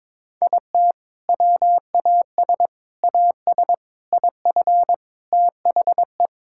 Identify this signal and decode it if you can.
Morse code, heard as a keyed tone, about 22 wpm, 700 Hz. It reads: IT WAS AS IF THE
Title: Sentences from Top 1000 Words - Review Part 6 - 22wpm